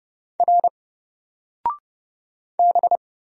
Morse code, heard as a keyed tone, about 30 wpm, 700 Hz. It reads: R B
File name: Single-Letter-Number-30wpm